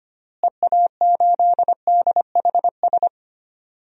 Morse code, heard as a keyed tone, about 25 wpm, 700 Hz. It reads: EA8DHS
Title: DXCC-100 Call Signs - 6 Characters - Repeated 2x - 25wpm